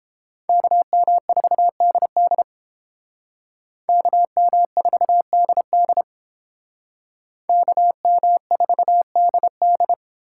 Morse code, heard as a keyed tone, about 30 wpm, 700 Hz. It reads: KM4DD KM4DD KM4DD